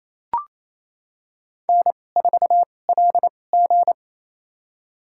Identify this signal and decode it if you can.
Morse code, heard as a keyed tone, about 28 wpm, spaced as for 20 wpm, 700 Hz. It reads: N4LG
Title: USA Call Signs - 4 Characters - 2x Character Spacing - 28wpm